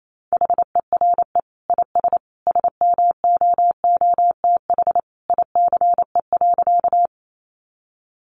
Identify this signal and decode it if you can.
Morse code, heard as a keyed tone, about 28 wpm, 700 Hz. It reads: HERE IS SMOOTH ICE.
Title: Sentences from Top 1400 Words 28wpm